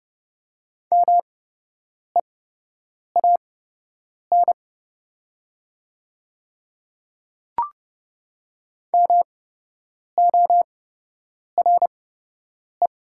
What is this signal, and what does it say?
Morse code, heard as a keyed tone, about 31 wpm, spaced as for 8 wpm, 700 Hz. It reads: MEAN MORE
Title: 8x Character Spacing - Top 300 Words - 30wpm